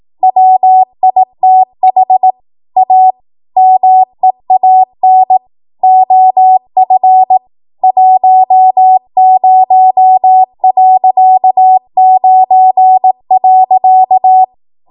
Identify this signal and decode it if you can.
Morse code, heard as a keyed tone, about 18 wpm, 750 Hz. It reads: WITH A MEAN OF 10.9.
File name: Morse / 18WPM